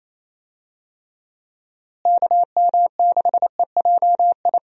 Morse code, heard as a keyed tone, about 27 wpm, 700 Hz. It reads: KM6EJI
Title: USA Call Signs - 6 Characters - Repeated 3x - 28wpm